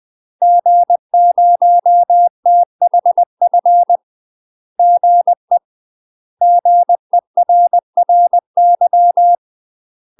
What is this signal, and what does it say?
Morse code, heard as a keyed tone, about 20 wpm, 700 Hz. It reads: G0THF GE GERRY
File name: UK MASTER QSOs TRAINING / 20WPM / UK MASTER QSOs 0068